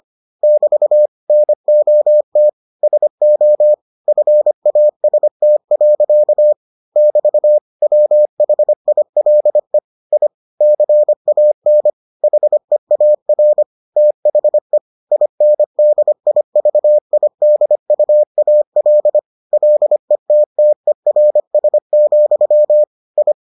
Morse code, heard as a keyed tone, about 25 wpm, 600 Hz. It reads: <BT> NOT SO FAST. <BT> WHILE I CAN HEAR THE INDIVIDUAL LETTERS, I